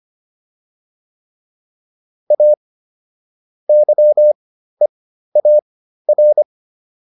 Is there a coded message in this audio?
Yes: Morse code from 2.3 to 6.4 s, heard as a keyed tone, about 25 wpm, spaced as for 13 wpm, 600 Hz. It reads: A YEAR